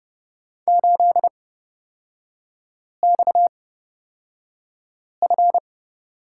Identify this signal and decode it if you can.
Morse code, heard as a keyed tone, about 30 wpm, spaced as for 5 wpm, 700 Hz. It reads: 8XF
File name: Lesson 37e - Letter x - Three Characters 30wpm